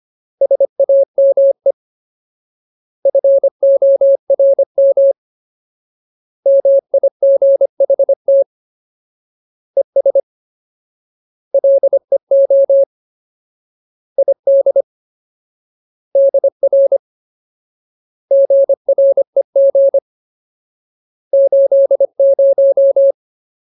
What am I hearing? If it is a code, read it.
Morse code, heard as a keyed tone, about 25 wpm, 550 Hz. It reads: SAME FORM MIGHT ES LEO ID DR GREG 80
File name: Ultimate Word List / 25WPM Ultimate